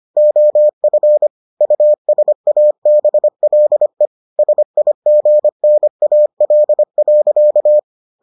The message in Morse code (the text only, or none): OF USABLE SIGNAL.